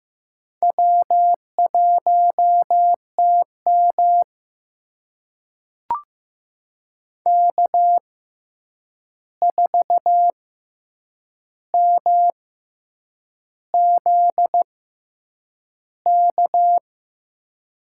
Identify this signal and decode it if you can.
Morse code, heard as a keyed tone, about 15 wpm, spaced as for 7 wpm, 700 Hz. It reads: W1TM K4MZK
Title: USA Call Signs - 6x Character Spacing - 15wpm